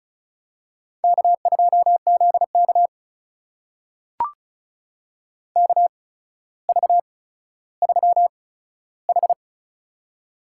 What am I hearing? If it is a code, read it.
Morse code, heard as a keyed tone, about 36 wpm, spaced as for 15 wpm, 700 Hz. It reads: K2ZK KV3H
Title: USA Call Signs - 4 Characters - 8x Character Spacing - 35wpm